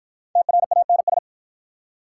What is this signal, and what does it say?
Morse code, heard as a keyed tone, about 52 wpm, 700 Hz. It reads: TRANS